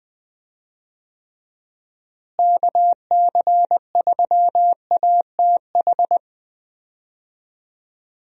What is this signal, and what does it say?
Morse code, heard as a keyed tone, about 20 wpm, 700 Hz. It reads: KC3ATH
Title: USA Call Signs - 6 Characters - Repeated 2x - 20wpm